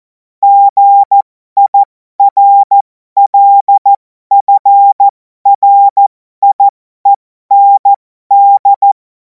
Morse code, heard as a keyed tone, about 14 wpm, spaced as for 12 wpm, 800 Hz. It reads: GIRLFRIEND
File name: Morsekurs / Witze